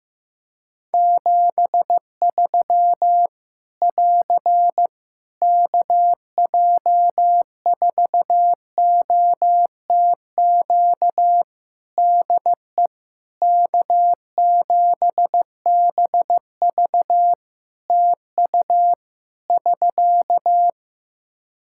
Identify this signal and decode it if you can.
Morse code, heard as a keyed tone, about 14 wpm, 700 Hz. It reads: 73 <AR> KJ4OTQ DE K7BV TU <SK>
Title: K1USN SST QSO Practice 15wpm